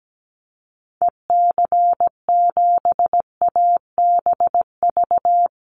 Morse code, heard as a keyed tone, about 17 wpm, 700 Hz. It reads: EC7ABV
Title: Call Signs - 17wpm